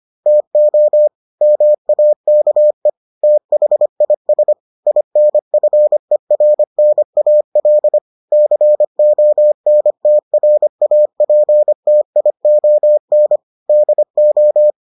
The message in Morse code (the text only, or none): TO MAKE THIS INFERNAL CONTRAPTION DO